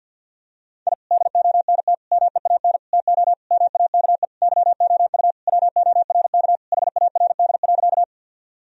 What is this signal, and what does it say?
Morse code, heard as a keyed tone, about 51 wpm, 700 Hz. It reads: I DONT MEAN TO MAKE YOU WORK HARD.
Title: Sentences from Top 1000 Words - Review Part 1 - 50wpm